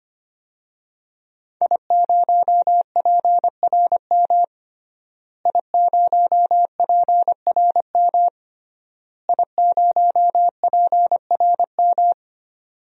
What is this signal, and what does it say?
Morse code, heard as a keyed tone, about 25 wpm, 700 Hz. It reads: I0PRM I0PRM I0PRM
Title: DXCC-100 Call Signs - 5 Characters - Repeated 3x - 25wpm